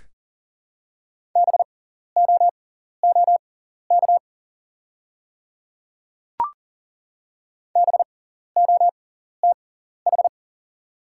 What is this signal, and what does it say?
Morse code, heard as a keyed tone, about 40 wpm, spaced as for 14 wpm, 690 Hz. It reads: BOOK BOTH
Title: 6x Character Spacing - Top 500 Words - 40wpm